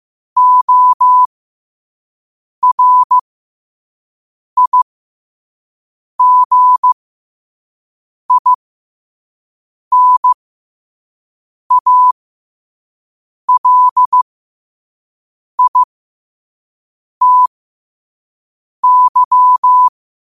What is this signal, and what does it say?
Morse code, heard as a keyed tone, about 15 wpm, spaced as for 5 wpm, 1000 Hz. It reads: ORIGINALITY
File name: qotdfiles / 5wpm